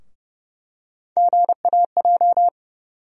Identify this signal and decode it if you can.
Morse code, heard as a keyed tone, about 30 wpm, 700 Hz. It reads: GAJ
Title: USA Call Sign Postfixes - 3 Letters 30wpm